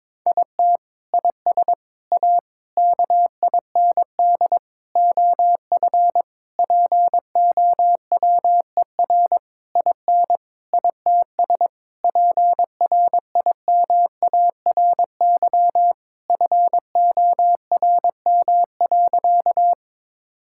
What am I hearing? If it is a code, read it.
Morse code, heard as a keyed tone, about 22 wpm, 700 Hz. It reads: IT IS A KIND OF POWER IN ITS PRIMARY FORM.
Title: Sentences from Top 1000 Words - Review Part 5 - 22wpm